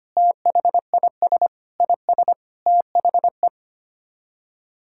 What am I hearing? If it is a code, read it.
Morse code, heard as a keyed tone, about 25 wpm, 700 Hz. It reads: THIS IS THE
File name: Sets of 9 Words - 25wpm